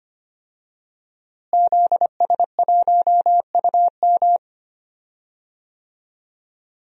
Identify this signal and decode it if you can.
Morse code, heard as a keyed tone, about 25 wpm, 700 Hz. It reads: ZS1UM
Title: DXCC-100 Call Signs - 5 Characters - Repeated 3x - 25wpm